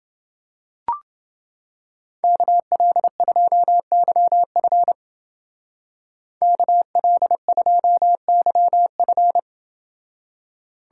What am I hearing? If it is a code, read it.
Morse code, heard as a keyed tone, about 28 wpm, 700 Hz. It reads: KL2YF KL2YF